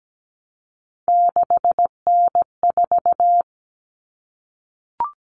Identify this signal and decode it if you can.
Morse code, heard as a keyed tone, about 17 wpm, 700 Hz. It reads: 6N4
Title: Common Cut Numbers - 3-Digit 17wpm